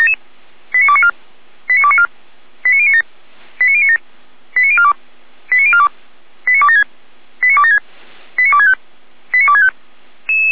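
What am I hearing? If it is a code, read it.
Five-tone selective call (ZVEI), heard as quick runs of five tones, tones of about 70 ms. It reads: 89294 89294 89097 89097 89042 89042 89276 89276 89265 89265
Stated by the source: ZVEI five-tone call received by SDR.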